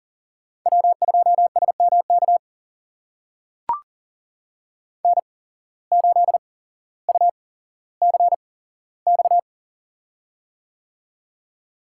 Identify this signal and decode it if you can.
Morse code, heard as a keyed tone, about 40 wpm, spaced as for 17 wpm, 700 Hz. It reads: W2SMK N8UCX
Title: USA Call Signs - 5 Characters - 8x Character Spacing - 40wpm